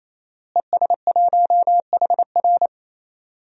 Morse code, heard as a keyed tone, about 28 wpm, 700 Hz. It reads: ES1HR
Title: DXCC-100 Call Signs - 5 Characters - Repeated 3x - 28wpm